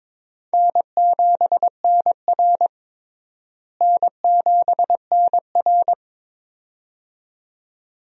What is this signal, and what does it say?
Morse code, heard as a keyed tone, about 21 wpm, 700 Hz. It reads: N7NR N7NR
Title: USA Call Signs - 4 Characters - Repeated 2x - 22wpm